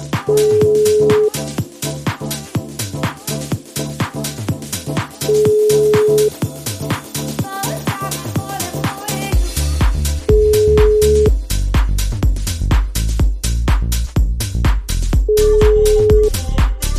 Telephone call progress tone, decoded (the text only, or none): ringback tone